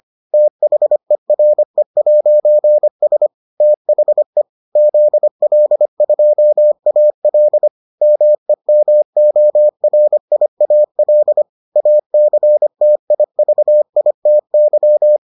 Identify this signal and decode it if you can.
Morse code, heard as a keyed tone, about 25 wpm, 600 Hz. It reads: THERE'S THE ZL2AL MEMORIAL ACTIVITY